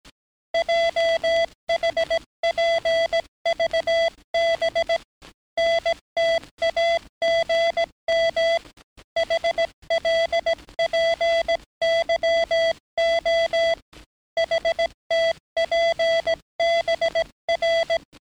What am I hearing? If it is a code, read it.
Morse code, heard as a keyed tone, about 18 wpm, 670 Hz. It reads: JHPVB NTAGM HLPYO HTPBR
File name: training / NAVPERS 11049to11060 / NAVPERS 11058 RA RB